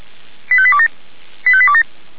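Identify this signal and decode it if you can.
Five-tone selective call (ZVEI), heard as quick runs of five tones, tones of about 70 ms. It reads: 85827 85827
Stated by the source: ZVEI five-tone selective call 85726, received via SDR.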